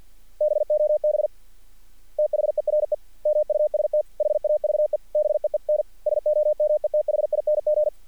Morse code, heard as a keyed tone, about 50 wpm, 600 Hz. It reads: BOB THERE MUST HAVE BEEN SOMETHING
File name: archived / CRSnet